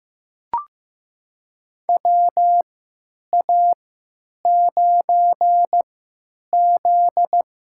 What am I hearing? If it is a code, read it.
Morse code, heard as a keyed tone, about 15 wpm, spaced as for 9 wpm, 700 Hz. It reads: WA9Z
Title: USA Call Signs - 6 Characters - 3x Character Spacing - 15wpm